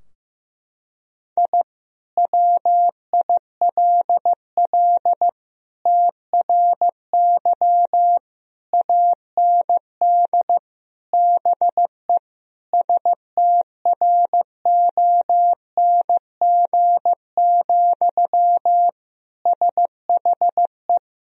Morse code, heard as a keyed tone, about 15 wpm, 700 Hz. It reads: I WILL TRY AND BE STRONG, SHE
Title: Sentences from Top 1000 Words - Review Part 3 - 15wpm